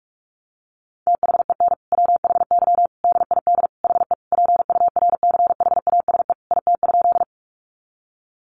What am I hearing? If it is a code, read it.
Morse code, heard as a keyed tone, about 44 wpm, 700 Hz. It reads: THEN WHY DID HE PURCHASE IT?
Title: Sentences from Top 800 Words 45wpm